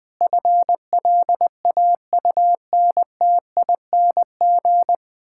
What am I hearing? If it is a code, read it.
Morse code, heard as a keyed tone, about 20 wpm, 700 Hz. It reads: FLAUNTING
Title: Pangram Challenge v2 - 20wpm